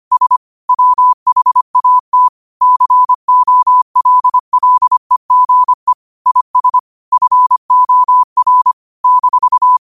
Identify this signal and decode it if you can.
Morse code, heard as a keyed tone, about 25 wpm, 1000 Hz. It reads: S WHAT COLLEGE IS FOR -
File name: qotdfiles / 25wpm